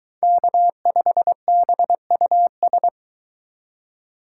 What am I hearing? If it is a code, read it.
Morse code, heard as a keyed tone, about 22 wpm, 700 Hz. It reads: K5BUS